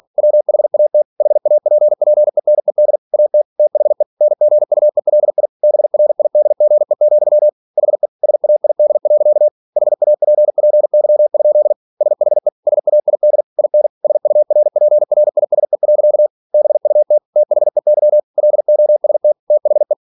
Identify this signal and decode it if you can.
Morse code, heard as a keyed tone, about 46 wpm, 600 Hz. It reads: WHAT HAPPENED AT THE NGUELI BRIDGE, HE SAID. HAPPY? SHE SAID IN SURPRISE. BUT THEY LOST THE
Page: Never 47wpm